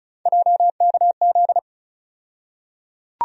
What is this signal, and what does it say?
Morse code, heard as a keyed tone, about 35 wpm, 700 Hz. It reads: JKZ
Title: USA Call Sign Postfixes - 3 Letters 35wpm